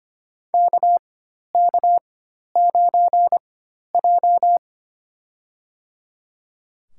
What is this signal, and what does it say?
Morse code, heard as a keyed tone, about 25 wpm, spaced as for 12 wpm, 700 Hz. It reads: KK9J
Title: USA Call Signs - 4 Characters - 4x Character Spacing - 25wpm